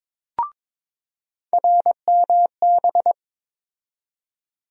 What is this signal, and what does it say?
Morse code, heard as a keyed tone, about 22 wpm, 700 Hz. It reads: RMB
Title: USA Call Sign Postfixes - 3 Letters 22wpm